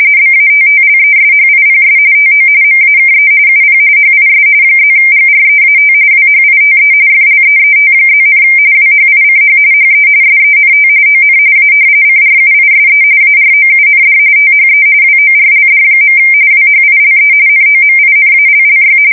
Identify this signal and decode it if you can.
RTTY, about 45.45 baud. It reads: BATTLESHIPS TO MAKE UP HIS +GREAT WHITE FLEET+, WHICH HE SENT ON A CRUISE AROUND THE WORLD. WHILE NOMINALLY
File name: fsk170 / usnavy3